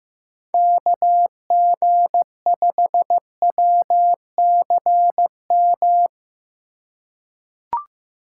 Morse code, heard as a keyed tone, about 15 wpm, 700 Hz. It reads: KG5WCM
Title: USA Call Signs - 3x Character Spacing - 15wpm